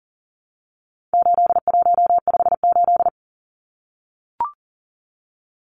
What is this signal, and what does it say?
Morse code, heard as a keyed tone, about 40 wpm, 700 Hz. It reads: 8158
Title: Numbers - 4-Digit 40wpm